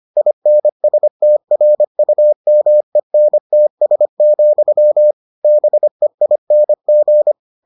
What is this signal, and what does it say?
Morse code, heard as a keyed tone, about 25 wpm, 600 Hz. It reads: INSTRUMENTS, BEING